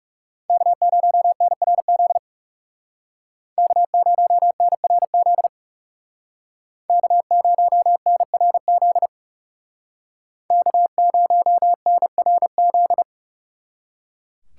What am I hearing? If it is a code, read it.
Morse code, heard as a keyed tone, about 38 wpm, 700 Hz. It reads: K0NRZ K0NRZ K0NRZ K0NRZ